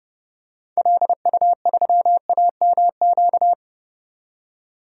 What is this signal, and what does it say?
Morse code, heard as a keyed tone, about 32 wpm, 700 Hz. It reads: LU3AMQ
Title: DXCC-100 Call Signs - 6 Characters - 30wpm